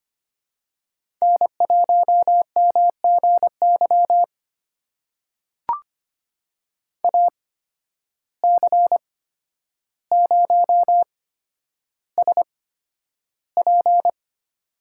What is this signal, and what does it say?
Morse code, heard as a keyed tone, about 25 wpm, spaced as for 11 wpm, 700 Hz. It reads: N1MGY AC0SP
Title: USA Call Signs - 8x Character Spacing - 25wpm